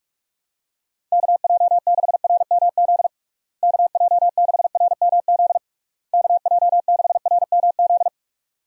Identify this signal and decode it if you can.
Morse code, heard as a keyed tone, about 46 wpm, 700 Hz. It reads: KJ6RMZ KJ6RMZ KJ6RMZ
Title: USA Call Signs - 6 Characters - Repeated 3x - 45wpm